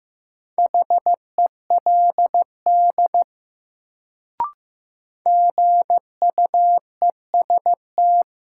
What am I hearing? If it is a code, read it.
Morse code, heard as a keyed tone, about 15 wpm, 700 Hz. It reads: HELD GUEST
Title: Sentences from Top 1600 Words - New Word Drill Alt - 15wpm